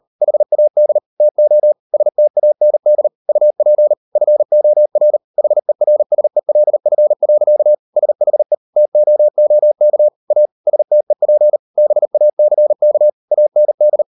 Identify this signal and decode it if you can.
Morse code, heard as a keyed tone, about 38 wpm, 600 Hz. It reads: HAD TO STAND UP FOR HERSELF. SHE TOOK A STEP BACK AND